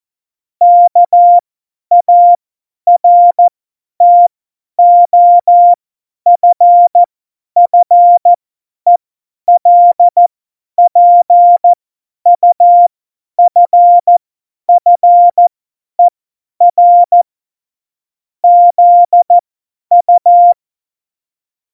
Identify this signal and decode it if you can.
Morse code, heard as a keyed tone, about 14 wpm, spaced as for 10 wpm, 700 Hz. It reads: KARTOFFELPUFFER ZU
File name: Morsekurs / Witze / Witz 06. Oktober 2020, 50 F70